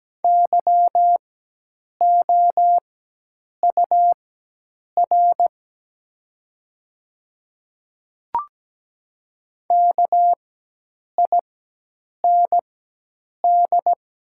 Morse code, heard as a keyed tone, about 17 wpm, spaced as for 8 wpm, 700 Hz. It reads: YOUR KIND
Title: 4x Character Spacing - Top 200 Words - 17wpm